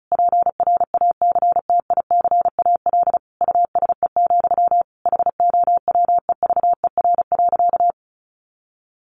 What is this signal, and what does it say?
Morse code, heard as a keyed tone, about 34 wpm, 700 Hz. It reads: PRACTICAL USE, HOWEVER.
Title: Sentences from Top 1400 Words 35wpm